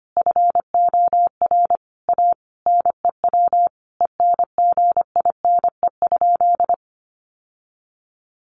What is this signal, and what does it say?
Morse code, heard as a keyed tone, about 25 wpm, 700 Hz. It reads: FOR A NEW ENGINE?
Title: Sentences from Top 1500 Words 25wpm